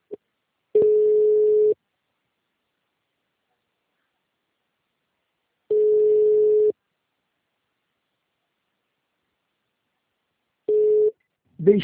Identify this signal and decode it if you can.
Ringback tone (Europe, 425 Hz), with speech.